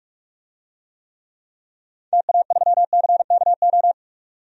Morse code, heard as a keyed tone, about 46 wpm, 700 Hz. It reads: TA3CKO
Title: DXCC-100 Call Signs - 6 Characters - 45wpm